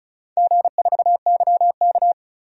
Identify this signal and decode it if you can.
Morse code, heard as a keyed tone, about 36 wpm, 700 Hz. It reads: G4YK
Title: DXCC-100 Call Signs - 4 Characters - 35wpm